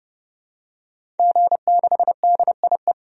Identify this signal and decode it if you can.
Morse code, heard as a keyed tone, about 30 wpm, 700 Hz. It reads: G6DIE
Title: DXCC-100 Call Signs - 5 Characters - 30wpm